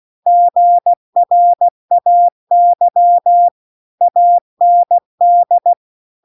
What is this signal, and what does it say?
Morse code, heard as a keyed tone, about 16 wpm, 700 Hz. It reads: GRAY AND